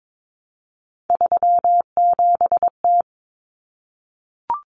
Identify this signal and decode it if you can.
Morse code, heard as a keyed tone, about 22 wpm, 700 Hz. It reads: 37T
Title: Common Cut Numbers - 3-Digit 22wpm